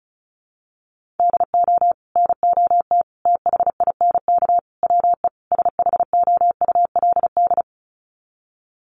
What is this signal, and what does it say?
Morse code, heard as a keyed tone, about 34 wpm, 700 Hz. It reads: DO NOT THINK WE SHOULD
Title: Sets of 5 Words Encore 35wpm